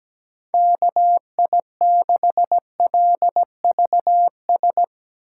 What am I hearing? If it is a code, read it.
Morse code, heard as a keyed tone, about 17 wpm, 700 Hz. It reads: KI6LVS
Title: USA Call Signs - 6 Characters - 2x Character Spacing - 17wpm